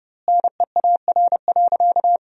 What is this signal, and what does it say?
Morse code, heard as a keyed tone, about 32 wpm, 700 Hz. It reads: NEAR.
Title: 2x Word-Spacing - Sentences from Top 200 Words - 30wpm